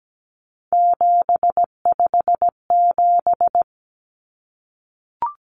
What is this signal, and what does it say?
Morse code, heard as a keyed tone, about 17 wpm, 700 Hz. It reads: 757
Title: Common Cut Numbers - 3-Digit 17wpm